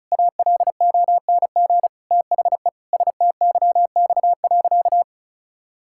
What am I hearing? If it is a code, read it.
Morse code, heard as a keyed tone, about 36 wpm, 700 Hz. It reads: ALONG THE STYX.
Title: Pangram Challenge v2 - 35wpm